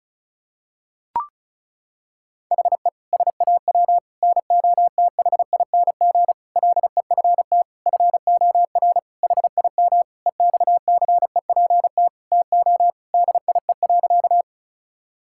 Morse code, heard as a keyed tone, about 35 wpm, 700 Hz. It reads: HE SAW NOTHING LEFT FOR HIM EXCEPT TO DIE.
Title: Sentences from Top 1600 Words 35wpm